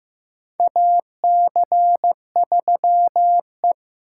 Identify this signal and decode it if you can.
Morse code, heard as a keyed tone, about 15 wpm, 700 Hz. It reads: AC3E
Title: USA Call Signs - 4 Characters - 6x Character Spacing - 15wpm